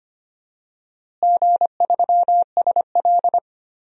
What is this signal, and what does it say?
Morse code, heard as a keyed tone, about 25 wpm, 700 Hz. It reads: G3SL
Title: DXCC-100 Call Signs - 4 Characters - Repeated 3x - 25wpm